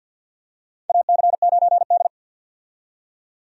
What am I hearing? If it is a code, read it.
Morse code, heard as a keyed tone, about 52 wpm, 700 Hz. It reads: AC9D
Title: USA Call Signs - 4 Characters - 50wpm